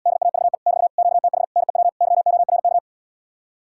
Morse code, heard as a keyed tone, about 77 wpm, 700 Hz. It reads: DAVE <BT> QTH NEW YORK